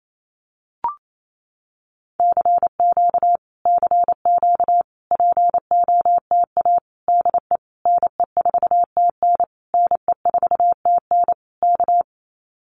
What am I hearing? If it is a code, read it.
Morse code, heard as a keyed tone, about 28 wpm, 700 Hz. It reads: CQ CQ POTA DE NE4TN NE4TN K
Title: Common QSO Phrases Encore 28wpm